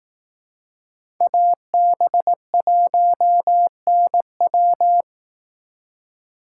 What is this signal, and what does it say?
Morse code, heard as a keyed tone, about 18 wpm, 700 Hz. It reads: AB1NW
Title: Speed Racing - USA 2x2 Call Signs 15wpm